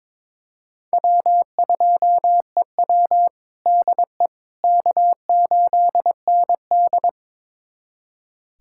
Morse code, heard as a keyed tone, about 22 wpm, 700 Hz. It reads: W2EW DE K8ND
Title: USA Callsign DE Callsign - 4 Characters - 22wpm